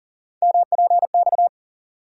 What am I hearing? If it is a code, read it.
Morse code, heard as a keyed tone, about 40 wpm, 700 Hz. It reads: MPX
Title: USA Call Sign Postfixes - 3 Letters 40wpm